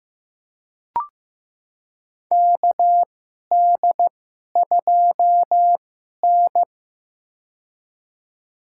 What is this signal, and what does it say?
Morse code, heard as a keyed tone, about 15 wpm, spaced as for 11 wpm, 700 Hz. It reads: KD2N
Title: USA Call Signs - 4 Characters - 2x Character Spacing - 15wpm